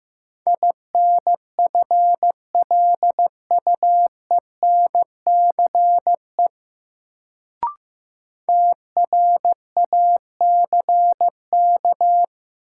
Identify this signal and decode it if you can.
Morse code, heard as a keyed tone, about 15 wpm, 700 Hz. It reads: INFLUENCE TRACK
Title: Sentences from Top 1100 Words - New Word Drill - 15wpm